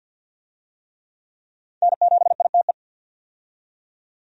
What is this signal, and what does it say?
Morse code, heard as a keyed tone, about 51 wpm, 700 Hz. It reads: N7ITE
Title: USA Call Signs - 5 Characters - 50wpm